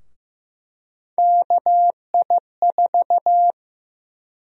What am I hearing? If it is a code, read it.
Morse code, heard as a keyed tone, about 15 wpm, 700 Hz. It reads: KI4
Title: USA Call Sign Prefixes - Repeated 2x - 15wpm